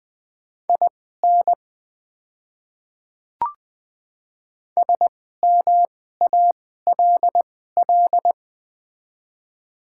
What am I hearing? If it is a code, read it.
Morse code, heard as a keyed tone, about 20 wpm, spaced as for 14 wpm, 700 Hz. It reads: IN SMALL
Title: 2x Character Spacing - Top 200 Words - 20wpm